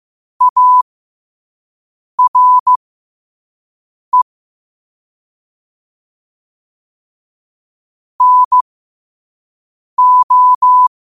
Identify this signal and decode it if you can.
Morse code, heard as a keyed tone, about 15 wpm, spaced as for 5 wpm, 1000 Hz. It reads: ARE NO
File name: qotdfiles / 5wpm